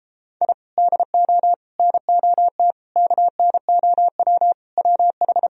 Morse code, heard as a keyed tone, about 33 wpm, 700 Hz. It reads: I DO NOT KNOW WH